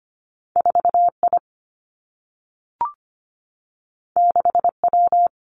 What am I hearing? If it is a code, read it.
Morse code, heard as a keyed tone, about 25 wpm, 700 Hz. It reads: 4I 6W
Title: Two Challenging Letter-Number Combinations 25wpm